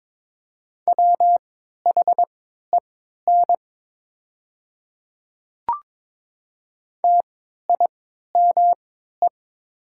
Morse code, heard as a keyed tone, about 22 wpm, spaced as for 12 wpm, 700 Hz. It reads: WHEN TIME
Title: 3x Character Spacing - Top 100 Words - 22wpm